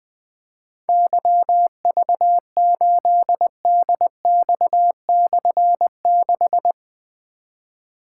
Morse code, heard as a keyed tone, about 20 wpm, 700 Hz. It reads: YV8DX/6
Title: Difficult Call Signs 20wpm